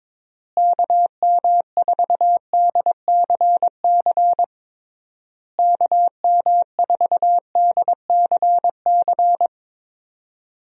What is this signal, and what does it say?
Morse code, heard as a keyed tone, about 22 wpm, 700 Hz. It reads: KM4DCC KM4DCC
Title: USA Call Signs - 6 Characters - Repeated 3x - 22wpm